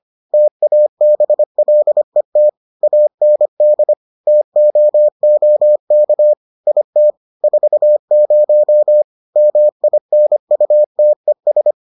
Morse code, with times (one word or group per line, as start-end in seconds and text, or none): 0.3-2.5 TABLET
2.8-3.9 AND
4.3-6.3 TOOK
6.7-7.1 IT
7.4-9.0 40
9.4-11.7 MINUTES